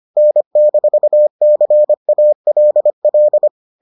Morse code, heard as a keyed tone, about 25 wpm, 600 Hz. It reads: N-CALL